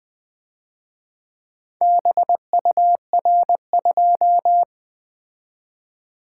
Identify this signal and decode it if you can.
Morse code, heard as a keyed tone, about 24 wpm, 700 Hz. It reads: BUR2
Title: Speed-Racing - Four Letter-Number Combinations - 20wpm